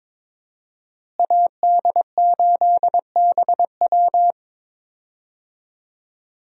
Morse code, heard as a keyed tone, about 21 wpm, 700 Hz. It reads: AD8BW
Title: USA Call Signs - 5 Characters - Repeated 2x - 22wpm